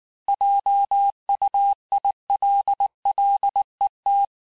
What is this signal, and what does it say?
Morse code, heard as a keyed tone, about 19 wpm, 800 Hz. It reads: JUILLET
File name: cours-du-lundi files / cours-1589-texte-clair-vitesse-18mpm-puis-23mpm